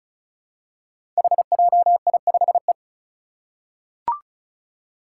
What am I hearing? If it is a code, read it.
Morse code, heard as a keyed tone, about 34 wpm, 700 Hz. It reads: HJI5E
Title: Five Challenging Letter-Number Combinations 35wpm